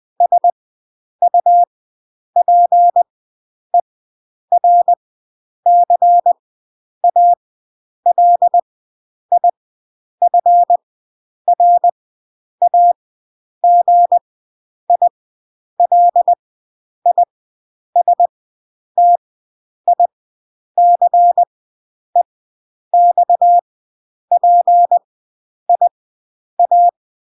Morse code, heard as a keyed tone, about 20 wpm, spaced as for 9 wpm, 700 Hz. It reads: SUPERCALIFRAGILISTICEXPIA